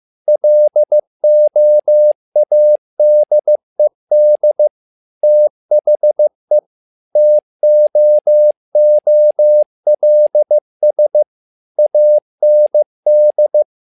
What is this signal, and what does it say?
Morse code, heard as a keyed tone, about 15 wpm, 600 Hz. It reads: LOADED THE TOOLS AND
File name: learning center / tower trouble / tower trouble 15wpm